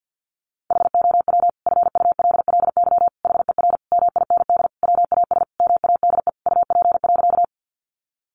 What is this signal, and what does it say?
Morse code, heard as a keyed tone, about 50 wpm, 700 Hz. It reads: HOW FULLY HER MIND WAS MADE UP.
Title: Sentences from Top 1300 Words 50wpm